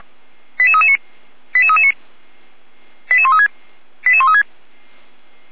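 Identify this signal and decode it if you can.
Five-tone selective call (ZVEI), heard as quick runs of five tones, tones of about 70 ms. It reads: 88339 88339 88136 88136
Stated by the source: ZVEI five-tone selective call 88341 received by SDR, logged on a Feuerwehr Orlamünde fire brigade server.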